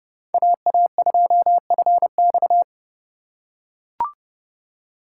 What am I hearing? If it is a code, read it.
Morse code, heard as a keyed tone, about 31 wpm, 700 Hz. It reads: AA2FX
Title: USA Call Signs - 5 Characters - 3x Character Spacing - 30wpm